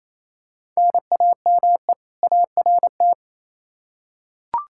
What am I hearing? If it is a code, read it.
Morse code, heard as a keyed tone, about 28 wpm, 700 Hz. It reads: NAME ART
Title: K1USN SST QSO Practice 28wpm